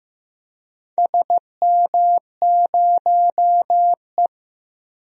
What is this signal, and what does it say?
Morse code, heard as a keyed tone, about 15 wpm, 700 Hz. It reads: SM0E
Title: DXCC-100 Call Signs - 4 Characters - Repeated 3x - 15wpm